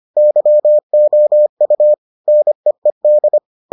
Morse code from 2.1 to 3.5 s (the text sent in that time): NEED